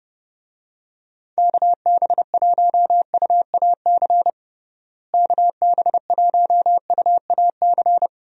Morse code, heard as a keyed tone, about 31 wpm, 700 Hz. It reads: KB1UAC KB1UAC
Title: USA Call Signs - 6 Characters - Repeated 3x - 30wpm